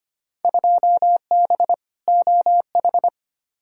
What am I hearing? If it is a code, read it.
Morse code, heard as a keyed tone, about 25 wpm, 700 Hz. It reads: 2B OH
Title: Field Day Exchanges - 25wpm